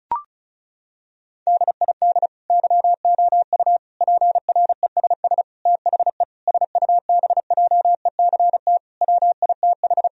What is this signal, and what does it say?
Morse code, heard as a keyed tone, about 36 wpm, 700 Hz. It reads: DID YOU PRESS THE SUBJECT WITH
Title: Sentences from Top 1000 Words - Review Part 1 - 35wpm